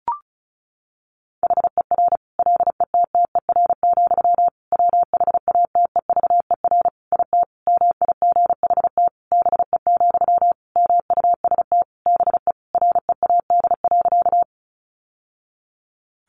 Morse code, heard as a keyed tone, about 34 wpm, 700 Hz. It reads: HER LETTER, WHATEVER IT MIGHT BE, MUST BE READ.
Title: Sentences from Top 1100 Words 35wpm